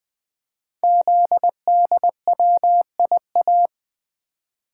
Morse code, heard as a keyed tone, about 20 wpm, 700 Hz. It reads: ZDWIA
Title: Code Groups - 5 Letters - 20wpm